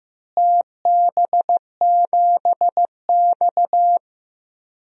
Morse code, heard as a keyed tone, about 15 wpm, 700 Hz. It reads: TB7X
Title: DXCC-100 Call Signs - 4 Characters - Repeated 2x - 15wpm